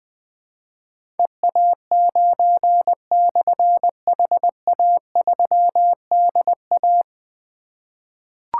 Morse code, heard as a keyed tone, about 20 wpm, 700 Hz. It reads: EA9/HA3DA
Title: Difficult Call Signs 20wpm